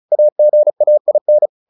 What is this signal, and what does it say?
Morse code, heard as a keyed tone, about 34 wpm, 600 Hz. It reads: AGAIN
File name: the old man and the sea / the old man and the sea 1 35wpm